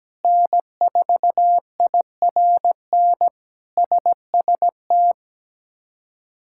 Morse code, heard as a keyed tone, about 17 wpm, 700 Hz. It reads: N4IRN SST
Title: K1USN SST QSO Practice - 17wpm